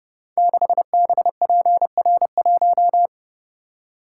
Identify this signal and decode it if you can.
Morse code, heard as a keyed tone, about 30 wpm, 700 Hz. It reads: 6BPR1
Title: Code Groups - 5 Letter:Numbers - 30wpm